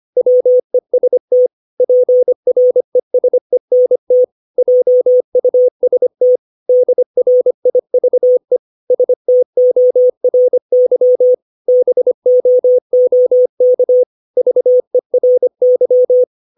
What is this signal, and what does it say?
Morse code, heard as a keyed tone, about 25 wpm, 500 Hz. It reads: WEST PRESENT JUST DRIVE STORY BOOK VERY
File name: 632 words / 632 Words 25WPM